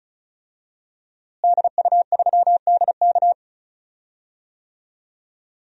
Morse code, heard as a keyed tone, about 34 wpm, 700 Hz. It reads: DU3DK
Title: DXCC-100 Call Signs - 5 Characters - Repeated 3x - 35wpm